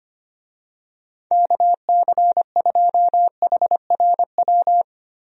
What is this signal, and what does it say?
Morse code, heard as a keyed tone, about 25 wpm, 700 Hz. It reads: KC2HRW
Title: USA Call Signs - 6 Characters - Repeated 3x - 25wpm